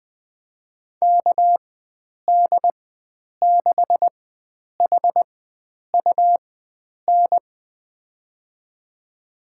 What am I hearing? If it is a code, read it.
Morse code, heard as a keyed tone, about 20 wpm, spaced as for 9 wpm, 700 Hz. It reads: KD6HUN
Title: USA Call Signs - 6 Characters - 4x Character Spacing - 20wpm